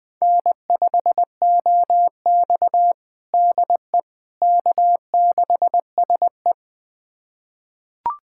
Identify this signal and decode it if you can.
Morse code, heard as a keyed tone, about 21 wpm, 700 Hz. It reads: N5OX DE K6SE
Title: USA Callsign DE Callsign - 4 Characters - 20wpm